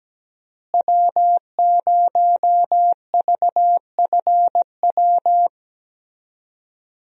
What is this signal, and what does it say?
Morse code, heard as a keyed tone, about 17 wpm, 700 Hz. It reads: W0VFW
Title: USA Call Signs - 5 Characters - Repeated 2x - 17wpm